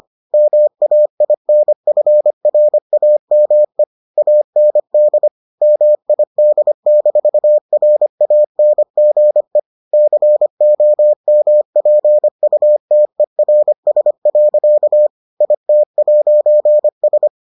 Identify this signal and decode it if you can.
Morse code, heard as a keyed tone, about 25 wpm, 600 Hz. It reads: MAINFRAME AND MID-RANGE COMPUTERS. IT'S